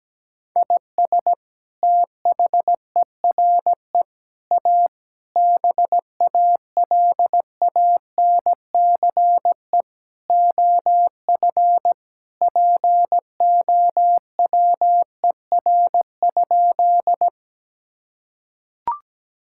Morse code, heard as a keyed tone, about 17 wpm, 700 Hz. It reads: IS THERE A BALANCE OF POWER?
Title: Sentences from Top 1000 Words - Review Part 6 - 17wpm